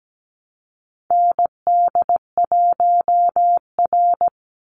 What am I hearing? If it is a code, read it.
Morse code, heard as a keyed tone, about 17 wpm, 700 Hz. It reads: ND1R
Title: Call Signs - 17wpm